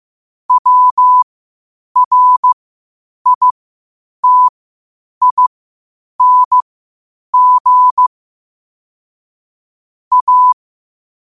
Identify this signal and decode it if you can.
Morse code, heard as a keyed tone, about 15 wpm, spaced as for 8 wpm, 1000 Hz. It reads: WRITING A